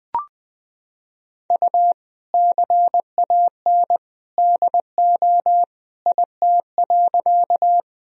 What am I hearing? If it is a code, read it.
Morse code, heard as a keyed tone, about 21 wpm, 700 Hz. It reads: U CAN DO IT.
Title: Sentences from Top 100 Words and Abbreviations 20wpm